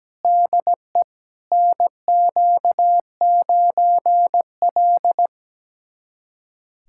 Morse code, heard as a keyed tone, about 17 wpm, 700 Hz. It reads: DE NQ9L
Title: USA Callsign DE Callsign - 4 Characters - 17wpm